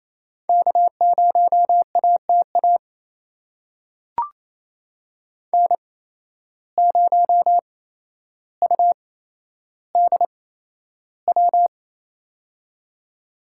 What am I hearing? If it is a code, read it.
Morse code, heard as a keyed tone, about 28 wpm, spaced as for 12 wpm, 700 Hz. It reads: K0ATA N0UDW
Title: USA Call Signs - 5 Characters - 8x Character Spacing - 28wpm